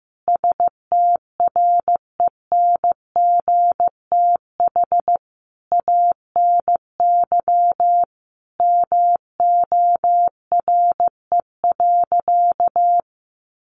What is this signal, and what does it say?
Morse code, heard as a keyed tone, about 15 wpm, 700 Hz. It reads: STRENGTH ANY MORE.
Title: Sentences from Top 1300 Words 15wpm